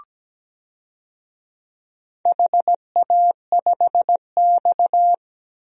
Morse code, heard as a keyed tone, about 17 wpm, 700 Hz. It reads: HA5X
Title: DXCC-100 Call Signs - 4 Characters - 17wpm